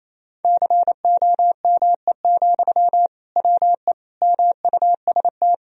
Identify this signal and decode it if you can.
Morse code, heard as a keyed tone, about 28 wpm, 700 Hz. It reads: COME, WE MUST
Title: Sentences from Top 1000 Words - Review Part 1 - 28wpm